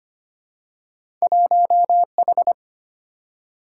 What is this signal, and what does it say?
Morse code, heard as a keyed tone, about 25 wpm, 700 Hz. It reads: 1H
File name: Two Letter-Number Combinations - 25wpm